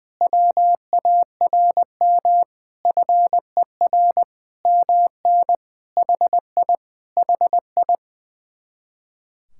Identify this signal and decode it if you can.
Morse code, heard as a keyed tone, about 20 wpm, 700 Hz. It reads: WARM FER MN HI HI
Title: Speed-Racing Common QSO Phrases - 20wpm